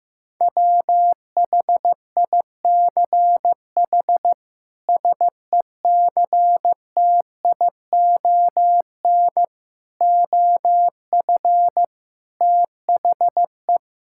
Morse code, heard as a keyed tone, about 15 wpm, 700 Hz. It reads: WHICH SECTION OF THE
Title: Sentences from Top 1000 Words - Review Part 5 - 15wpm